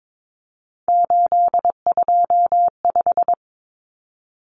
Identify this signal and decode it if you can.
Morse code, heard as a keyed tone, about 22 wpm, 700 Hz. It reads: 825
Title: Common Cut Numbers - 3-Digit 22wpm